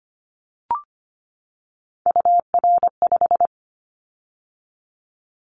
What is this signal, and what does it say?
Morse code, heard as a keyed tone, about 25 wpm, 700 Hz. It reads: UR5
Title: Callsign Prefixes for Top 25 DXCC - 25wpm